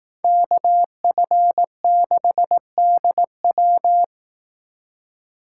Morse code, heard as a keyed tone, about 18 wpm, 700 Hz. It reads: KF6DW
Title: Speed Racing - USA 2x2 Call Signs 15wpm